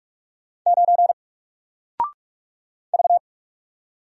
Morse code, heard as a keyed tone, about 44 wpm, 700 Hz. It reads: 9 V
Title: Single Letter-Number Alt 45wpm